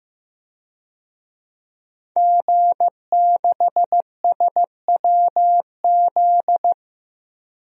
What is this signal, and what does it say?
Morse code, heard as a keyed tone, about 15 wpm, 700 Hz. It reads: G6SWZ
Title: DXCC-100 Call Signs - 5 Characters - Repeated 2x - 15wpm